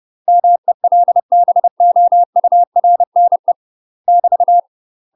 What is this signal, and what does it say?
Morse code, heard as a keyed tone, about 30 wpm, 700 Hz. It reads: MELBOURNE <BT>
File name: UK MASTER QSOs TRAINING / 30WPM / UK MASTER QSOs 0046